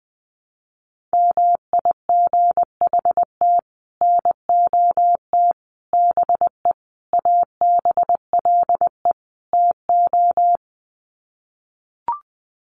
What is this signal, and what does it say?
Morse code, heard as a keyed tone, about 20 wpm, 700 Hz. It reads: MIGHT NOT BE ABLE TO
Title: Sets of 5 Words Encore2 20wpm